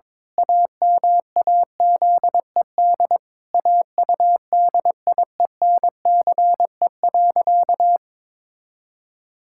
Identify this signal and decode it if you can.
Morse code, heard as a keyed tone, about 22 wpm, 700 Hz. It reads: AMAZED AUDIENCE.
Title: Pangram Challenge v2 - 22wpm